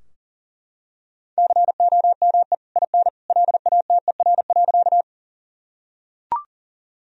Morse code, heard as a keyed tone, about 40 wpm, 700 Hz. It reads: COME IN LATER.
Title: Sentences from Top 1000 Words - Review Part 4 - 40wpm